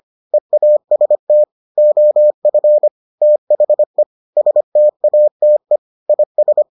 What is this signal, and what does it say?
Morse code, heard as a keyed tone, about 25 wpm, 600 Hz. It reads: EAST OF THE STATE IS